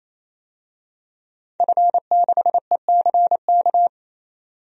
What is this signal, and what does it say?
Morse code, heard as a keyed tone, about 27 wpm, 700 Hz. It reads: F6ECK
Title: DXCC-100 Call Signs - 5 Characters - 28wpm